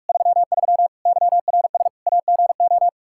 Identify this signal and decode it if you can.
Morse code, heard as a keyed tone, about 46 wpm, 700 Hz. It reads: 33 YRS AGO
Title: K1USN SST QSO Practice 45wpm